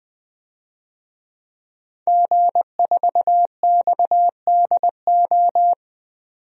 Morse code, heard as a keyed tone, about 20 wpm, 700 Hz. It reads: G4XDO
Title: DXCC-100 Call Signs - 5 Characters - 20wpm